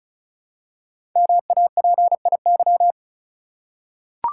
Morse code, heard as a keyed tone, about 36 wpm, 690 Hz. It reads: MAPIY
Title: Code Groups - 5 Letters - 35wpm